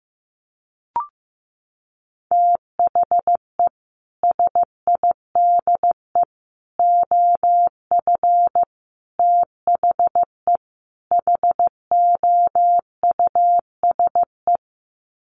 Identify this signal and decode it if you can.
Morse code, heard as a keyed tone, about 15 wpm, 700 Hz. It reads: THE SIDE OF THE HOUSE
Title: Sets of 5 Words Encore2 15wpm